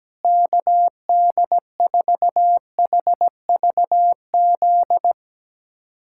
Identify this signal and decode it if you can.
Morse code, heard as a keyed tone, about 17 wpm, 700 Hz. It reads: KD4HVZ
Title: USA Callsign DE Callsign 17wpm